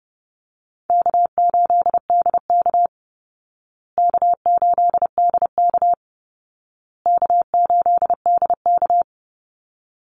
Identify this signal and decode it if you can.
Morse code, heard as a keyed tone, about 30 wpm, 700 Hz. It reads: K8DK K8DK K8DK
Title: Call Signs Repeated 2x - 30wpm